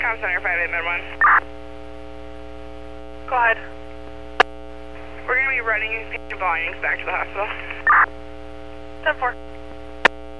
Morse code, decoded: none

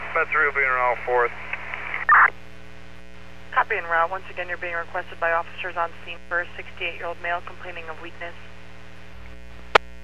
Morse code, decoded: none

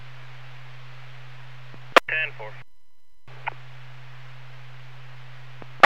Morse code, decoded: none

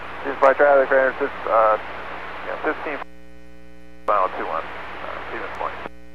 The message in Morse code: none

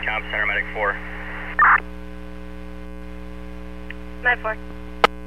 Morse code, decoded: none

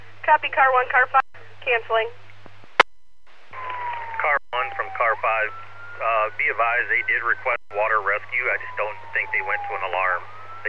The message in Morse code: none

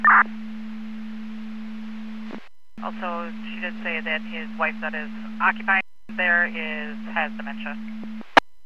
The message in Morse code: none

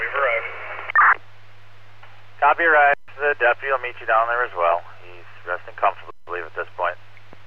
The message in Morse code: none

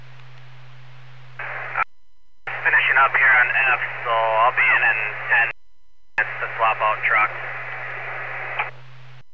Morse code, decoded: none

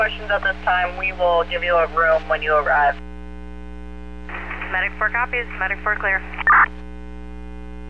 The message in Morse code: none

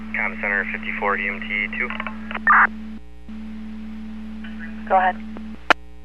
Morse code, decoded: none

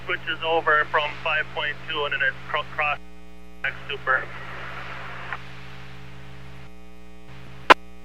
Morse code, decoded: none

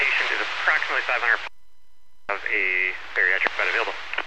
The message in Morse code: none

